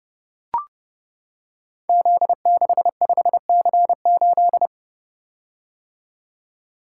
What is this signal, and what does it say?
Morse code, heard as a keyed tone, about 30 wpm, 700 Hz. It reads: Z65C8
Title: Code Groups - 5 Letter:Numbers - 30wpm